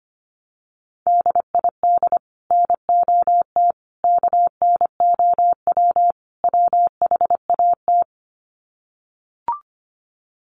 Morse code, heard as a keyed tone, about 25 wpm, 700 Hz. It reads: DID NOT KNOW WHAT
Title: Sets of 4 Words 25wpm